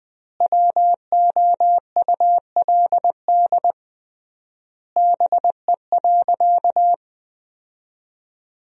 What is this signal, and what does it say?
Morse code, heard as a keyed tone, about 20 wpm, 700 Hz. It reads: WOULD BE.
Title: 3x Word-Spacing - Sentences from Top 200 Words - 20wpm